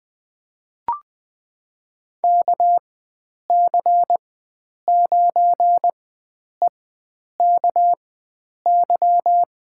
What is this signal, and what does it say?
Morse code, heard as a keyed tone, about 20 wpm, spaced as for 9 wpm, 700 Hz. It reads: KC9EKY
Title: USA Call Signs - 6 Characters - 4x Character Spacing - 20wpm